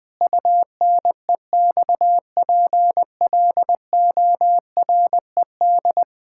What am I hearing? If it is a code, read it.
Morse code, heard as a keyed tone, about 21 wpm, 700 Hz. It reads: UNEXPLORED
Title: Difficult Sentences with Hints - 20wpm